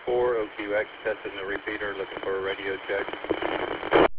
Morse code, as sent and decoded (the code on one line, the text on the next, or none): none